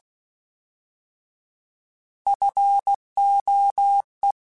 Morse code, heard as a keyed tone, about 16 wpm, 800 Hz. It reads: FOE
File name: Alphabetical / Alphabetical mp3s / Fs @15